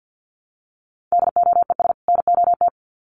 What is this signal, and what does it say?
Morse code, heard as a keyed tone, about 50 wpm, 700 Hz. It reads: DOES NOT
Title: Sets of 2 Words 50wpm